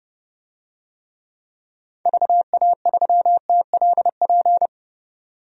Morse code, heard as a keyed tone, about 32 wpm, 700 Hz. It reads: VA3TLP
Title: DXCC-100 Call Signs - 6 Characters - 30wpm